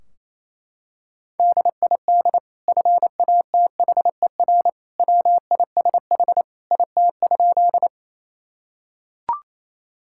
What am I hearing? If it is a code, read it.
Morse code, heard as a keyed tone, about 28 wpm, 700 Hz. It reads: DID FATHER WISH IT?
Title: Sentences from Top 1600 Words 28wpm